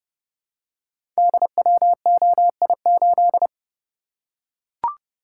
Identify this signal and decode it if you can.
Morse code, heard as a keyed tone, about 28 wpm, 700 Hz. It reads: DWOI8
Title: Code Groups - 5 Letter:Numbers - 30wpm